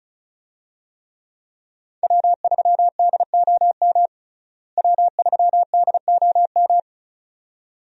Morse code, heard as a keyed tone, about 34 wpm, 700 Hz. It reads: W3DOM W3DOM
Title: USA Call Signs - 5 Characters - Repeated 2x - 35wpm